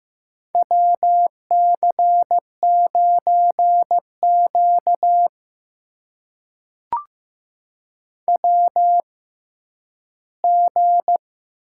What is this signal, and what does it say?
Morse code, heard as a keyed tone, about 15 wpm, spaced as for 10 wpm, 700 Hz. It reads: WC9Q WG
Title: USA Call Signs - 4 Characters - 6x Character Spacing - 15wpm